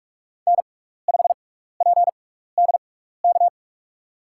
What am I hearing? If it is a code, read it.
Morse code, heard as a keyed tone, about 42 wpm, spaced as for 15 wpm, 700 Hz. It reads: N5PDK
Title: USA Call Signs - 5 Characters - 6x Character Spacing - 45wpm